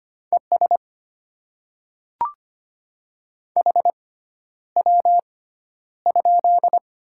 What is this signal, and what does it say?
Morse code, heard as a keyed tone, about 25 wpm, spaced as for 11 wpm, 690 Hz. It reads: ES HW?
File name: Top 20 QSO Elements - 6x Character Spacing - 25wpm